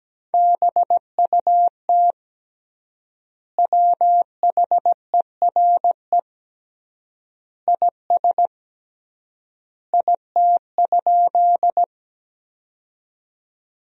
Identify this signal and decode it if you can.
Morse code, heard as a keyed tone, about 17 wpm, 700 Hz. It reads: BUT WHERE IS IT?
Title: 3x Word-Spacing - Sentences from Top 200 Words - 17wpm